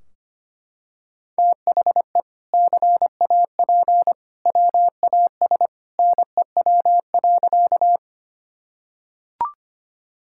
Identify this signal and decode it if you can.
Morse code, heard as a keyed tone, about 25 wpm, 700 Hz. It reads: THE CAP WAS NEW.
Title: Sentences from Top 1600 Words 25wpm